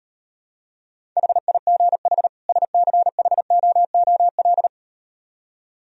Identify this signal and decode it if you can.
Morse code, heard as a keyed tone, about 38 wpm, 700 Hz. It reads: HIGH SCHOOL